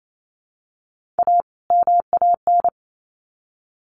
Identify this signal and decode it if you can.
Morse code, heard as a keyed tone, about 28 wpm, 700 Hz. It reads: A MAN
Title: Sets of 2 Words 28wpm